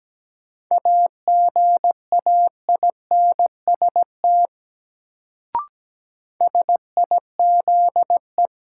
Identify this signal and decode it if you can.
Morse code, heard as a keyed tone, about 17 wpm, 700 Hz. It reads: AGAINST SIZE
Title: Sentences from Top 400 Words - New Word Drill - 17wpm